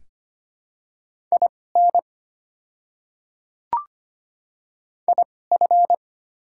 Morse code, heard as a keyed tone, about 25 wpm, spaced as for 18 wpm, 690 Hz. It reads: IN IF
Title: 2x Character Spacing - Top 100 Words - 25wpm